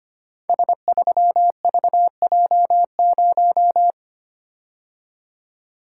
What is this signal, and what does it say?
Morse code, heard as a keyed tone, about 25 wpm, 700 Hz. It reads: S3VJ0
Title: Code Groups - 5 Letter:Numbers - 25wpm